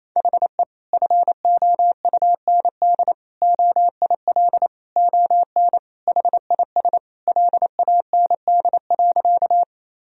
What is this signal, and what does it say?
Morse code, heard as a keyed tone, about 27 wpm, 700 Hz. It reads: HE FOUND OIL ON HIS LAND.
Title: Sentences from Top 1000 Words - Review Part 4 - 28wpm